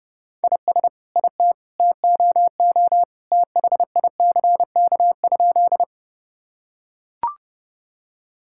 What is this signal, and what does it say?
Morse code, heard as a keyed tone, about 32 wpm, 700 Hz. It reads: IS IT TOO THICK?
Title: Sentences from Top 1600 Words 30wpm